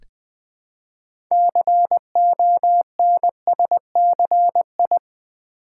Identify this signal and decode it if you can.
Morse code, heard as a keyed tone, about 20 wpm, 700 Hz. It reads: CONSCI
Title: Build A Word - 3 Letters At A Time - 9-Letter Words - 20wpm